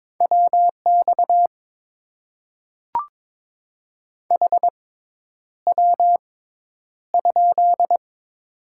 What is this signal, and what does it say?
Morse code, heard as a keyed tone, about 22 wpm, spaced as for 10 wpm, 700 Hz. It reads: WX HW?
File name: Top 90 QSO Elements - 6x Character Spacing - 22wpm